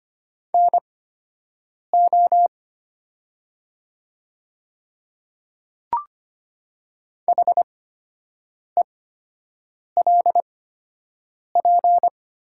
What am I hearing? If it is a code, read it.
Morse code, heard as a keyed tone, about 25 wpm, spaced as for 7 wpm, 700 Hz. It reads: NO HELP
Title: 8x Character Spacing - Top 200 Words - 25wpm